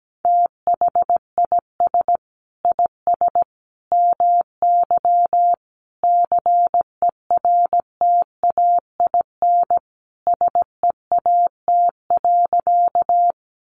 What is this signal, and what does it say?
Morse code, heard as a keyed tone, about 17 wpm, 700 Hz. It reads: THIS IS MY CERTAIN SEAT.
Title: Sentences from Top 1500 Words 17wpm